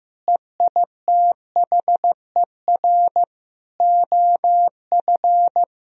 Morse code, heard as a keyed tone, about 15 wpm, 700 Hz. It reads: EITHER OF
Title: Sentences from Top 1000 Words - Review Part 4 - 15wpm